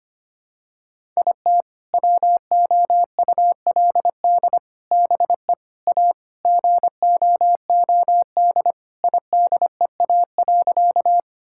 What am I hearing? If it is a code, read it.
Morse code, heard as a keyed tone, about 25 wpm, 700 Hz. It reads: IT WOULD BE A GOOD IDEA.
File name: practice / 25wpm / Sentences with 3-Word Phrases - 25wpm